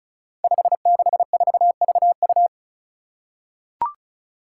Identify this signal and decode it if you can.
Morse code, heard as a keyed tone, about 34 wpm, 700 Hz. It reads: 564VU
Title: Five Challenging Letter-Number Combinations 35wpm